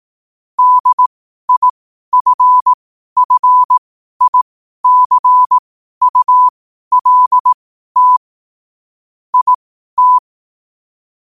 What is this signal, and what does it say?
Morse code, heard as a keyed tone, about 18 wpm, spaced as for 12 wpm, 1000 Hz. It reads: DIFFICULT IT
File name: qotdfiles / 12wpm